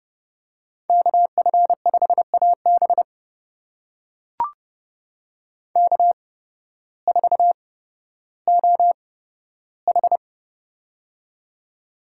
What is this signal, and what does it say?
Morse code, heard as a keyed tone, about 30 wpm, spaced as for 14 wpm, 700 Hz. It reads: KF5AB K4OH
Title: USA Call Signs - 8x Character Spacing - 30wpm